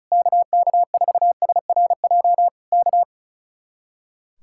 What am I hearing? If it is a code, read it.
Morse code, heard as a keyed tone, about 34 wpm, 700 Hz. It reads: KK4SRJ K
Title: Speed-Racing Common QSO Phrases Encore - 35wpm